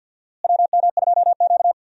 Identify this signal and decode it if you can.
Morse code, heard as a keyed tone, about 48 wpm, 700 Hz. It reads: WM2Q
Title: USA Call Signs - 4 Characters - Repeated 2x - 50wpm